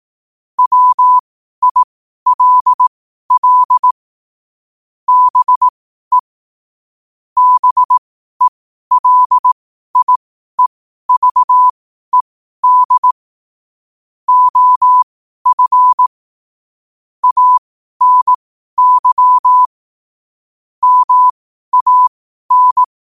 Morse code, heard as a keyed tone, about 18 wpm, spaced as for 12 wpm, 1000 Hz. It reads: WILL BE BELIEVED OF ANY MAN
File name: qotdfiles / 12wpm